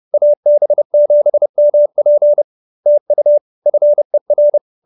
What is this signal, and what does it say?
Morse code, heard as a keyed tone, about 30 wpm, 590 Hz. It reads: AB7MP TU FER